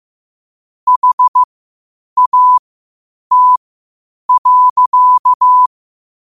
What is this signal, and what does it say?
Morse code, heard as a keyed tone, about 15 wpm, spaced as for 8 wpm, 1000 Hz. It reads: HAT.
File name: qotdfiles / 8wpm